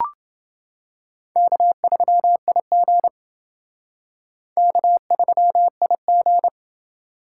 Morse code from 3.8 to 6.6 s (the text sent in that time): K3IG